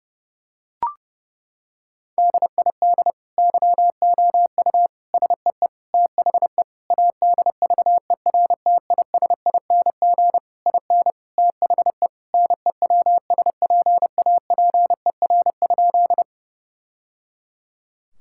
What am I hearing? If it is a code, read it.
Morse code, heard as a keyed tone, about 30 wpm, 700 Hz. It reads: DID YOU SEE THE ADVERTISING IN THE NEWSPAPER?
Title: Sentences from Top 1000 Words - Review Part 6 - 30wpm